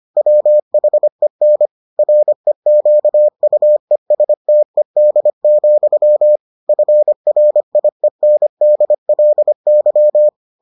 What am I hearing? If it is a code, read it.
Morse code, heard as a keyed tone, about 25 wpm, 600 Hz. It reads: WHEN REQUESTED, FRIENDLY